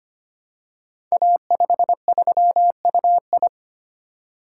Morse code, heard as a keyed tone, about 25 wpm, 700 Hz. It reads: A53UI
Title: Code Groups - 5 Letter:Numbers - 25wpm